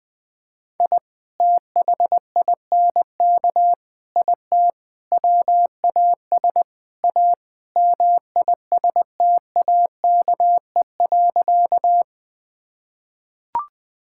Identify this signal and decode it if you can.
Morse code, heard as a keyed tone, about 20 wpm, 700 Hz. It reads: I THINK IT WAS A MISTAKE.
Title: Sentences with 5-Word Phrases - Spoken Beginning - 20wpm